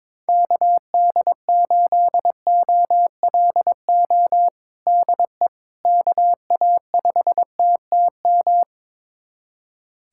Morse code, heard as a keyed tone, about 22 wpm, 700 Hz. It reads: KD8OLO DE KA5TTM
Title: USA Callsign DE Callsign 22wpm